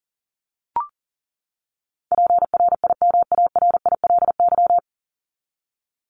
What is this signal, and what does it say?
Morse code, heard as a keyed tone, about 37 wpm, 700 Hz. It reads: PRIMARILY
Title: LY Suffix - Top 100 Words with Base Words 40wpm